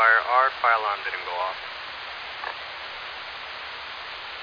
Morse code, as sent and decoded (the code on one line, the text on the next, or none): none